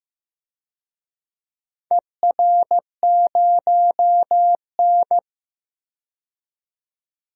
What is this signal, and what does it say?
Morse code, heard as a keyed tone, about 15 wpm, 700 Hz. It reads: ER0N
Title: DXCC-100 Call Signs - 4 Characters - 15wpm